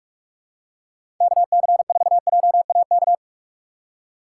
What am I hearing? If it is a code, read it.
Morse code, heard as a keyed tone, about 46 wpm, 700 Hz. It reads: KC4JAK
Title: USA Call Signs - 6 Characters - 6x Character Spacing - 45wpm